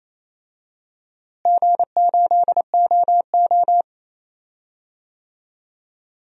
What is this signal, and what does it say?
Morse code, heard as a keyed tone, about 28 wpm, 700 Hz. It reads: G8OO
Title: DXCC-100 Call Signs - 4 Characters - Repeated 2x - 28wpm